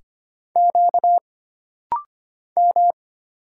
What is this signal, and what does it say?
Morse code, heard as a keyed tone, about 25 wpm, 700 Hz. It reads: Q M
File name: Single Character (Letter:Number:Punctuation:Prosign) - Rapid-Fire - 25wpm